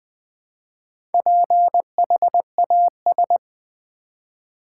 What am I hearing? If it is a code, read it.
Morse code, heard as a keyed tone, about 20 wpm, 700 Hz. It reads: PHAS
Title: Build A Word - 1 Letter At A Time - 5-Letter Words - 20wpm